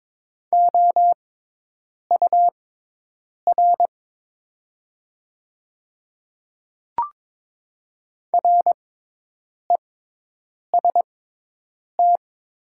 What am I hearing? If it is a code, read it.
Morse code, heard as a keyed tone, about 22 wpm, spaced as for 8 wpm, 690 Hz. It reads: OUR REST
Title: 6x Character Spacing - Top 500 Words - 22wpm